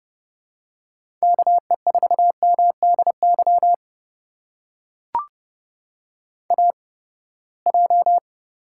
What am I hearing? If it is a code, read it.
Morse code, heard as a keyed tone, about 30 wpm, spaced as for 21 wpm, 700 Hz. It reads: KE4MDY AJ
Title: USA Call Signs - 8x Character Spacing - 30wpm